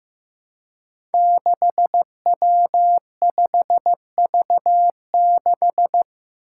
Morse code, heard as a keyed tone, about 15 wpm, 700 Hz. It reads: 6W5V6
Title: Five Challenging Letter-Number Combinations 15wpm